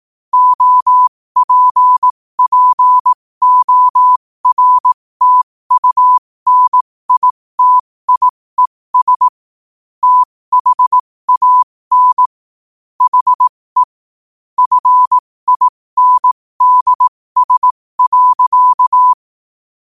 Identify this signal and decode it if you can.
Morse code, heard as a keyed tone, about 18 wpm, spaced as for 15 wpm, 1000 Hz. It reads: OPPORTUNITIES THAN HE FINDS.
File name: qotdfiles / 15wpm